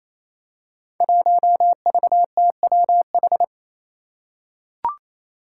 Morse code, heard as a keyed tone, about 28 wpm, 700 Hz. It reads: 1VTWH
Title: Five Challenging Letter-Number Combinations 28wpm